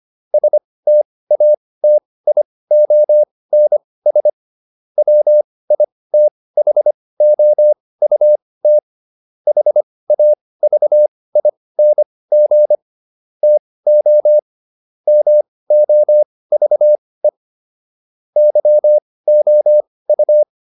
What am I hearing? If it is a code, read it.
Morse code, heard as a keyed tone, about 25 wpm, spaced as for 18 wpm, 600 Hz. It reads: STATIONS WITHOUT HAVING TO MOVE YOU